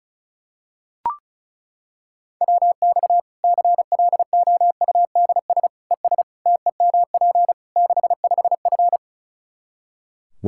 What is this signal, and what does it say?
Morse code, heard as a keyed tone, about 36 wpm, 700 Hz. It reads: WX CLOUDS ES TEMP 65F
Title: Practice QSO - Exchange 2 - Info - 35wpm